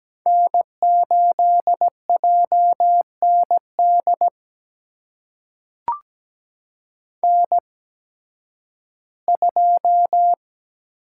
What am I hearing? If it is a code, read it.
Morse code, heard as a keyed tone, about 17 wpm, spaced as for 11 wpm, 700 Hz. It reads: N8JND N2
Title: USA Call Signs - 5 Characters - 8x Character Spacing - 17wpm